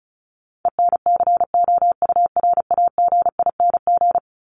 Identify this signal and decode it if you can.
Morse code, heard as a keyed tone, about 34 wpm, 700 Hz. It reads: ENCOURAGING
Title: Top 1751-2000 Words 35wpm